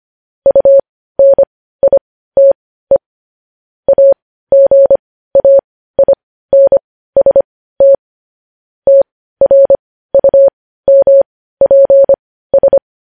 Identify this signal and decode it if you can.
Morse code, heard as a keyed tone, about 25 wpm, spaced as for 15 wpm, 550 Hz. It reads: UNITE AGAINST TRUMPS
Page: news headlines in morse code at 15 wpm